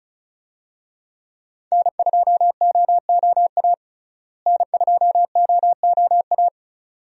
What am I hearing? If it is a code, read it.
Morse code, heard as a keyed tone, about 36 wpm, 700 Hz. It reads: N2OOA N2OOA
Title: USA Call Signs - 5 Characters - Repeated 2x - 35wpm